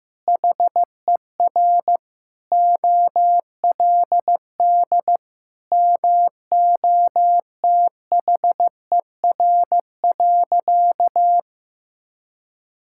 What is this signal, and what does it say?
Morse code, heard as a keyed tone, about 15 wpm, 700 Hz. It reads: HER OLD MOTHER.
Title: Sentences from Top 1000 Words - Review Part 5 - 15wpm